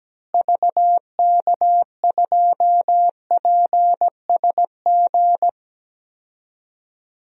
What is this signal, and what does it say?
Morse code, heard as a keyed tone, about 17 wpm, 700 Hz. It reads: VK2PSG
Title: DXCC-100 Call Signs - 6 Characters - Repeated 2x - 17wpm